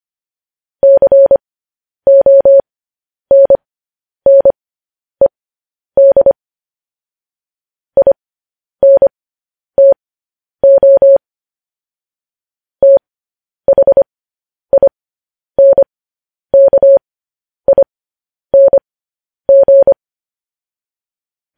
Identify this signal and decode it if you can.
Morse code, heard as a keyed tone, about 25 wpm, spaced as for 10 wpm, 550 Hz. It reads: CONNED INTO THINKING
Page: news headlines in morse code at 10 wpm